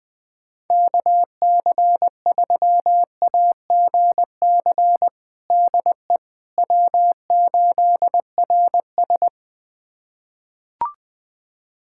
Morse code, heard as a keyed tone, about 21 wpm, 700 Hz. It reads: KC3AGC DE W8RS
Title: USA Callsign DE Callsign 20wpm